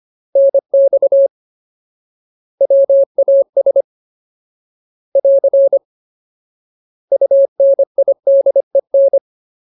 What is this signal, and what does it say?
Morse code, heard as a keyed tone, about 25 wpm, 550 Hz. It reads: NX WAS <AR> UNIDEN
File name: Ultimate Word List / 25WPM Ultimate